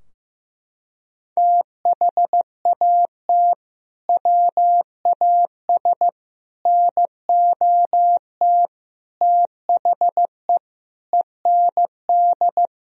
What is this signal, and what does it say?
Morse code, heard as a keyed tone, about 15 wpm, 700 Hz. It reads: THAT WAS NOT THE END
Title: Sentences from Top 1000 Words - Review Part 2 - 15wpm